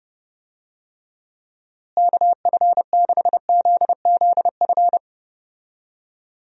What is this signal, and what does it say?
Morse code, heard as a keyed tone, about 32 wpm, 700 Hz. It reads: KF6ZZF
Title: USA Call Signs - 6 Characters - Repeated 3x - 30wpm